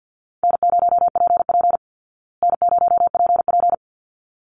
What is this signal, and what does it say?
Morse code, heard as a keyed tone, about 48 wpm, 700 Hz. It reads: N0PP N0PP
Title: Easy Call Signs Repeated 2x - 50wpm